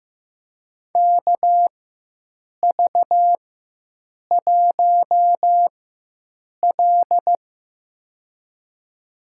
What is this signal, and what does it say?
Morse code, heard as a keyed tone, about 16 wpm, spaced as for 7 wpm, 700 Hz. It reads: KV1L
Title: USA Call Signs - 4 Characters - 4x Character Spacing - 15wpm